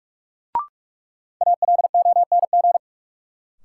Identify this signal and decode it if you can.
Morse code, heard as a keyed tone, about 47 wpm, 700 Hz. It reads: ALONG
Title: Sentences from Top 700 Words - New Word Drill - 45wpm